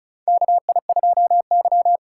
Morse code, heard as a keyed tone, about 34 wpm, 700 Hz. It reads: KI2Y
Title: USA Call Signs - 4 Characters - 4x Character Spacing - 35wpm